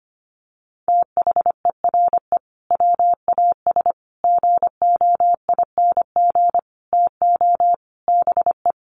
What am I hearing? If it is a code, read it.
Morse code, heard as a keyed tone, about 25 wpm, 700 Hz. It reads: THERE WAS GOING TO BE
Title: Sets of 5 Words Encore2 25wpm